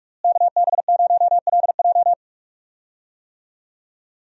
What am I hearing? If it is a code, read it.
Morse code, heard as a keyed tone, about 46 wpm, 700 Hz. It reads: KB0LJ
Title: USA Call Signs - 5 Characters - Repeated 2x - 45wpm